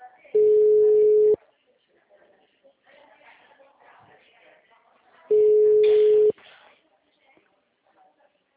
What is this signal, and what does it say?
Ringback tone (Europe, 425 Hz), with speech.
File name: out- -2018- - - .28259